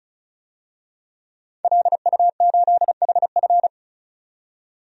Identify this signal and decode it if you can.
Morse code, heard as a keyed tone, about 34 wpm, 700 Hz. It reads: LU8HF
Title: DXCC-100 Call Signs - 5 Characters - 35wpm